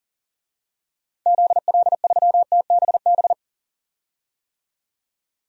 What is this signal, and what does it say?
Morse code, heard as a keyed tone, about 39 wpm, 700 Hz. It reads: ZL3TBB
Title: DXCC-100 Call Signs - 6 Characters - 40wpm